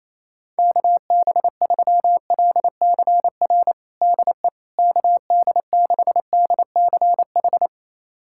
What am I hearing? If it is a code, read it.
Morse code, heard as a keyed tone, about 28 wpm, 700 Hz. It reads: KB3LCR DE KD6DCH
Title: USA Callsign DE Callsign - 6 Characters - 28wpm